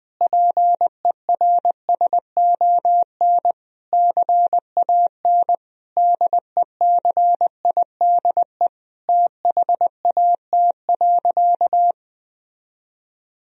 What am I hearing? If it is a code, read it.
Morse code, heard as a keyed tone, about 20 wpm, 700 Hz. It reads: PERSON CAN DECIDE THAT.
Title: Sentences from Top 1000 Words - Review Part 3 - 20wpm